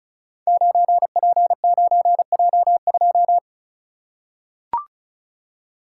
Morse code, heard as a keyed tone, about 36 wpm, 700 Hz. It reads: 9P9J2
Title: Code Groups - 5 Letter:Numbers - 35wpm